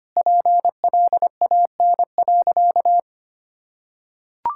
Morse code, heard as a keyed tone, about 25 wpm, 700 Hz. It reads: PLAN.
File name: practice / 25wpm / Sentences with 4-Word Phrases - 25wpm